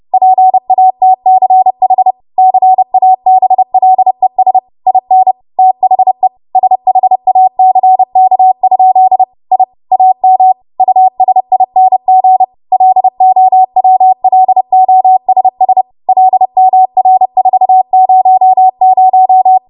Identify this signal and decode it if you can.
Morse code, heard as a keyed tone, about 30 wpm, 750 Hz. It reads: PATCH CABLES IN THE SHACK? I AM USING LOWLOSS LMR400
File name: Morse / 30WPM